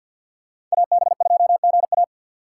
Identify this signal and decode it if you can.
Morse code, heard as a keyed tone, about 53 wpm, 700 Hz. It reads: AB2GA
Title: USA Call Signs - 5 Characters - 2x Character Spacing - 50wpm